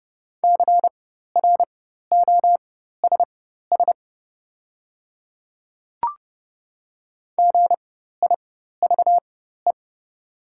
Morse code, heard as a keyed tone, about 31 wpm, spaced as for 14 wpm, 700 Hz. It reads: CROSS GIVE
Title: 4x Character Spacing - Top 300 Words - 30wpm